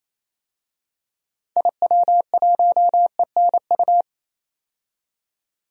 Morse code, heard as a keyed tone, about 28 wpm, 700 Hz. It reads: IW1ENU
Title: DXCC-100 Call Signs - 6 Characters - 28wpm